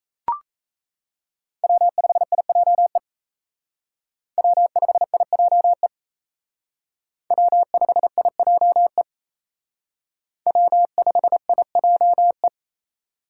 Morse code, heard as a keyed tone, about 35 wpm, 700 Hz. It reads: W5IJE W5IJE W5IJE W5IJE
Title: Speed Racing - USA 1x3 Call Signs 28wpm